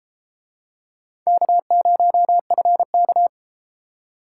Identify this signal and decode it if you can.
Morse code, heard as a keyed tone, about 32 wpm, 700 Hz. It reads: K0FK